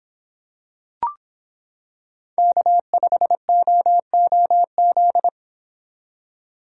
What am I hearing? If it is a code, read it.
Morse code, heard as a keyed tone, about 26 wpm, 700 Hz. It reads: K5OOZ